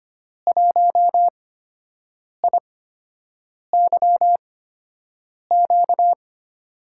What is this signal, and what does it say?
Morse code, heard as a keyed tone, about 25 wpm, spaced as for 7 wpm, 700 Hz. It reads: 1IYQ
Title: USA Call Signs - 5 Characters - 8x Character Spacing - 25wpm